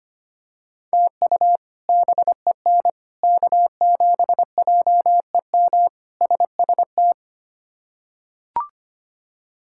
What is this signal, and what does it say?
Morse code, heard as a keyed tone, about 25 wpm, 700 Hz. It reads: TU BEN K7JEM SST
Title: K1USN SST QSO Practice - 25wpm